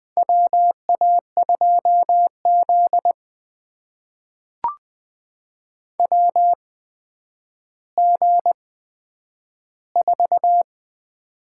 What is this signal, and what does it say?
Morse code, heard as a keyed tone, about 20 wpm, spaced as for 10 wpm, 700 Hz. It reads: WA2Z WG4
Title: USA Call Signs - 4 Characters - 8x Character Spacing - 20wpm